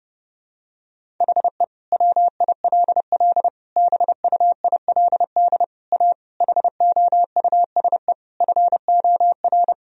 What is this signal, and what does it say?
Morse code, heard as a keyed tone, about 30 wpm, 700 Hz. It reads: HE WILL BUILD A HOUSE FOR
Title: Sentences from Top 1000 Words - Review Part 1 - 30wpm